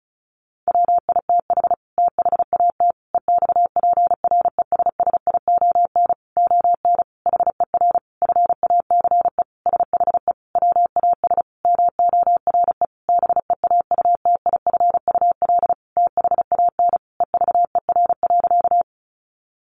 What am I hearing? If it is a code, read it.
Morse code, heard as a keyed tone, about 34 wpm, 700 Hz. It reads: WITH THAT EXPRESSION ON HER FACE SHE WAS MORE BEAUTIFUL THAN EVER.
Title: Sentences from Top 1200 Words 35wpm